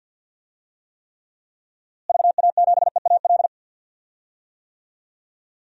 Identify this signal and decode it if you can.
Morse code, heard as a keyed tone, about 46 wpm, 700 Hz. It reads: VA7EAL
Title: DXCC-100 Call Signs - 6 Characters - Repeated 3x - 50wpm